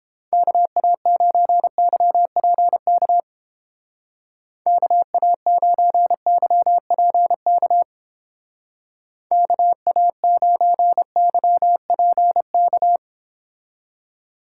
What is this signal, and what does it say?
Morse code, heard as a keyed tone, about 30 wpm, 700 Hz. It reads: KA9YPK KA9YPK KA9YPK